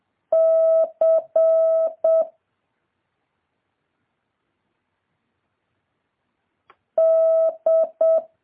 Morse code, heard as a keyed tone, about 7 wpm, 650 Hz. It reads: C D
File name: jh8cbh / 2023cw